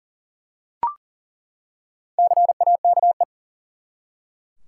Morse code, heard as a keyed tone, about 40 wpm, 700 Hz. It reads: CAKE
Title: Four Letter Words - 40wpm